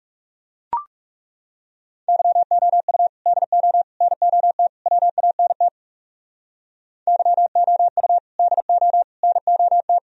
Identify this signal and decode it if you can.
Morse code, heard as a keyed tone, about 41 wpm, 700 Hz. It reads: YOU DO NOT WANT YOU DO NOT